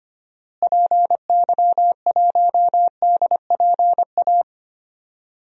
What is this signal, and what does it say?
Morse code, heard as a keyed tone, about 25 wpm, 700 Hz. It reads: PY1DPA
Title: DXCC-100 Call Signs - 6 Characters - Repeated 2x - 25wpm